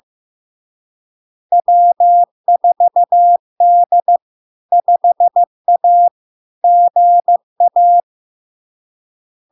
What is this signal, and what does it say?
Morse code, heard as a keyed tone, about 15 wpm, 700 Hz. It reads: W4D 5A GA
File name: LARK-FD-2022-Log-in-CW-15wpm-700-Hz